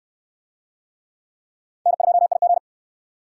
Morse code, heard as a keyed tone, about 60 wpm, 700 Hz. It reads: N2IG